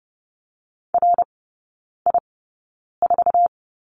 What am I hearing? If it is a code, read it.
Morse code, heard as a keyed tone, about 30 wpm, spaced as for 9 wpm, 700 Hz. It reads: RI4
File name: Lesson 16e - Letter c - Three Characters 30wpm